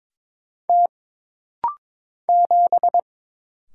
Morse code, heard as a keyed tone, about 22 wpm, 700 Hz. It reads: T 7
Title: Single Letter-Number Alt - Rapid-Fire 22wpm